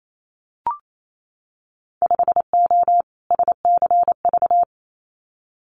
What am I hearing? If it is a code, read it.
Morse code, heard as a keyed tone, about 28 wpm, 700 Hz. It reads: 5O SCV
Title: Winter Field Day Exchanges Encore 28wpm